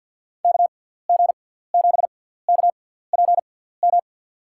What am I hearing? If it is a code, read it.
Morse code, heard as a keyed tone, about 50 wpm, spaced as for 17 wpm, 700 Hz. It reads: KG7KPM
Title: USA Call Signs - 6 Characters - 6x Character Spacing - 50wpm